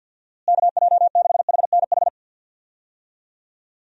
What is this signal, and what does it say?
Morse code, heard as a keyed tone, about 46 wpm, 700 Hz. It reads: KJ6HNH
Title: USA Call Signs - 6 Characters - Repeated 3x - 50wpm